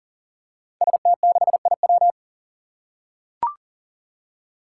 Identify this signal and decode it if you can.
Morse code, heard as a keyed tone, about 37 wpm, 690 Hz. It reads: ST6IW
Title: Five Challenging Letter-Number Combinations 40wpm